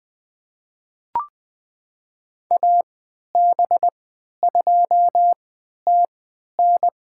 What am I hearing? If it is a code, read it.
Morse code, heard as a keyed tone, about 20 wpm, spaced as for 11 wpm, 700 Hz. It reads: AB2TN
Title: USA Call Signs - 5 Characters - 3x Character Spacing - 20wpm